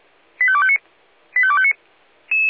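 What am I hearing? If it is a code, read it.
Five-tone selective call (ZVEI), heard as quick runs of five tones, tones of about 70 ms. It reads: 85259 85259
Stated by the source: Five-tone ZVEI selective call received by SDR, archived by the fire brigade Feuerwehr Orlamünde.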